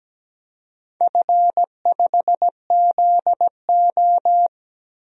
Morse code, heard as a keyed tone, about 17 wpm, 700 Hz. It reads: F5ZO
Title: DXCC-100 Call Signs - 4 Characters - 17wpm